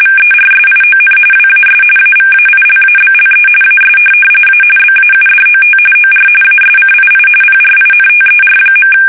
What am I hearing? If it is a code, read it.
RTTY, about 45.45 baud. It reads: VERSION WAS AVAILABLE AS A BAUDOT OR ASCII PRINTER.
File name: fsk850 / teletype-corp